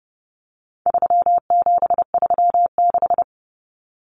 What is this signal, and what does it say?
Morse code, heard as a keyed tone, about 30 wpm, 700 Hz. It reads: 3736
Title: Numbers - 4-Digit 30wpm